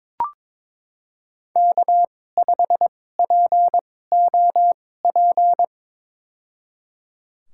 Morse code, heard as a keyed tone, about 22 wpm, spaced as for 16 wpm, 700 Hz. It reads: K5POP
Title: USA Call Signs - 2x Character Spacing - 22wpm